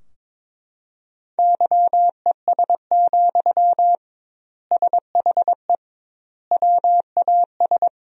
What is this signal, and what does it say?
Morse code, heard as a keyed tone, about 22 wpm, 700 Hz. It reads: YES, SHE WAS
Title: 2x Word-Spacing - Sentences from Top 500 Words - 22wpm